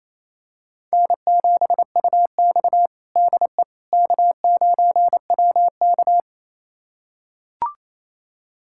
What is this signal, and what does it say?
Morse code, heard as a keyed tone, about 28 wpm, 700 Hz. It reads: N7UX DE K9WK
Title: USA Callsign DE Callsign - 4 Characters - 28wpm